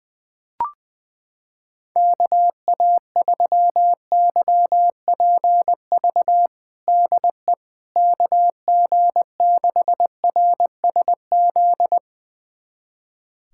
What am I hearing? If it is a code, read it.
Morse code, heard as a keyed tone, about 20 wpm, 700 Hz. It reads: KA3YPV DE KG6RSZ
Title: USA Callsign DE Callsign 20wpm